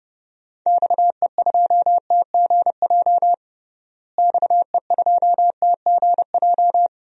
Morse code, heard as a keyed tone, about 32 wpm, 700 Hz. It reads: XE2TGJ XE2TGJ
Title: DXCC-100 Call Signs - 6 Characters - Repeated 2x - 30wpm